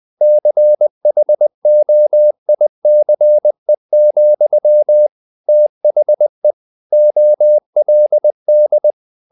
Morse code, heard as a keyed tone, about 20 wpm, 600 Hz. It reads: CHOICE, THE OLD
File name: the old man and the sea / the old man and the sea part 1 20wpm